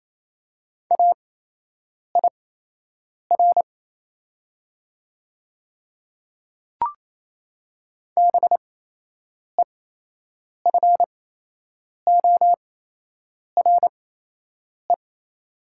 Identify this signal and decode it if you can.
Morse code, heard as a keyed tone, about 27 wpm, spaced as for 8 wpm, 700 Hz. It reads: AIR BEFORE
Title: 8x Character Spacing - Top 300 Words - 28wpm